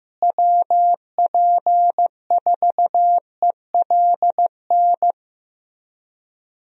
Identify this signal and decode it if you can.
Morse code, heard as a keyed tone, about 15 wpm, 700 Hz. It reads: WP4ELN
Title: USA Call Signs - 6 Characters - Repeated 3x - 15wpm